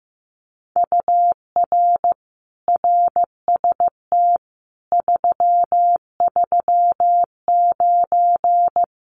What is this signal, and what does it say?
Morse code, heard as a keyed tone, about 15 wpm, 700 Hz. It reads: UR RST 339
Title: Common QSO Phrases Encore 15wpm